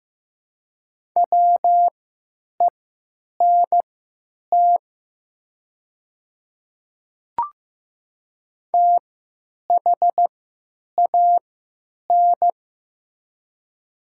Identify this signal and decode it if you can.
Morse code, heard as a keyed tone, about 15 wpm, spaced as for 9 wpm, 700 Hz. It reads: WENT THAN
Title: 3x Character Spacing - Top 200 Words - 15wpm